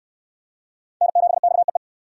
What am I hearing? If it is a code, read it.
Morse code, heard as a keyed tone, about 66 wpm, 700 Hz. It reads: N7XEE